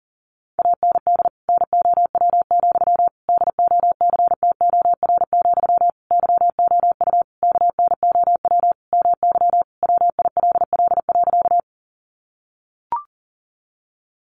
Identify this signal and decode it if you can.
Morse code, heard as a keyed tone, about 40 wpm, 700 Hz. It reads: AND NOW, DOCTOR, YOU KNOW MY WILL.
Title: Sentences from Top 1500 Words 40wpm